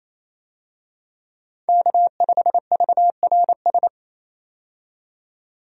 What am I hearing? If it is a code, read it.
Morse code, heard as a keyed tone, about 28 wpm, 700 Hz. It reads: K5VRS
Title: USA Call Signs - 5 Characters - 28wpm